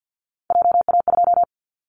Morse code, heard as a keyed tone, about 50 wpm, 700 Hz. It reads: JA2
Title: Callsign Prefixes for Top 25 DXCC - 50wpm